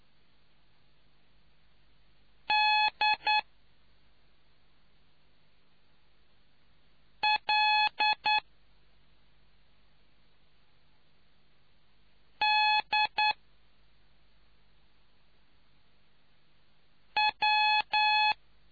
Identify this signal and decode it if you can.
Morse code, heard as a keyed tone, about 9 wpm, spaced as for 2 wpm, 850 Hz. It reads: DLDW